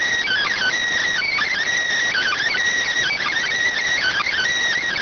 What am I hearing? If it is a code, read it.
Emergency Alert System (SAME) tones (header bursts), heard as harsh buzzing bursts. It is heard from 0.2 to 4.5 s.